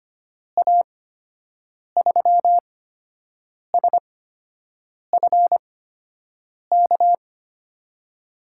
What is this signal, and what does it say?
Morse code, heard as a keyed tone, about 25 wpm, spaced as for 7 wpm, 700 Hz. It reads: A3SFK
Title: USA Call Signs - 8x Character Spacing - 25wpm